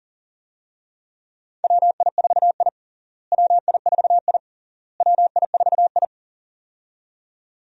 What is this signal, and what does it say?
Morse code, heard as a keyed tone, about 40 wpm, 700 Hz. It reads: WI4I WI4I WI4I
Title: USA Call Signs - 4 Characters - Repeated 3x - 40wpm